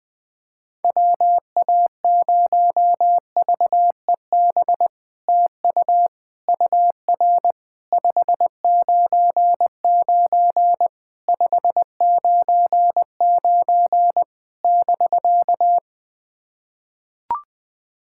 Morse code, heard as a keyed tone, about 21 wpm, 700 Hz. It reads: WA0VEB TU UR 599 599 <BK>
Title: SOTA QSO Practice 20wpm